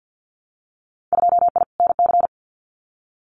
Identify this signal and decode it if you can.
Morse code, heard as a keyed tone, about 50 wpm, 700 Hz. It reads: 2I NC
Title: Winter Field Day Exchanges 50wpm